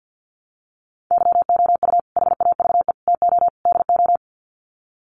Morse code, heard as a keyed tone, about 50 wpm, 700 Hz. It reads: YOU HAVE TO DO
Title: Sets of 4 Words 50wpm